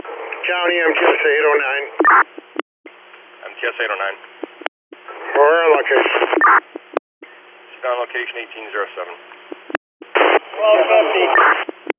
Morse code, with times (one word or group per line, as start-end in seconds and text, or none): none